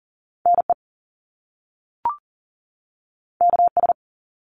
Morse code, heard as a keyed tone, about 41 wpm, 700 Hz. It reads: NE KS
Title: US States 40wpm